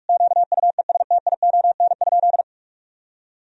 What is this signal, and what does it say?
Morse code, heard as a keyed tone, about 42 wpm, 700 Hz. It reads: QUESTION?
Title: Sentences from Top 1000 Words - Review Part 4 - 45wpm